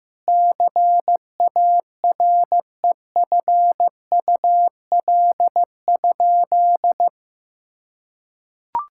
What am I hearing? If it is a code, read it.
Morse code, heard as a keyed tone, about 15 wpm, 700 Hz. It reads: CAREFUL?
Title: Sentences from Top 1600 Words 15wpm